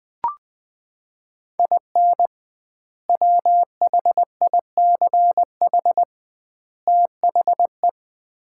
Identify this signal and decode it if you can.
Morse code, heard as a keyed tone, about 20 wpm, 700 Hz. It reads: IN WHICH THE
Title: 2x Word-Spacing - Sets of 3 Words - 20wpm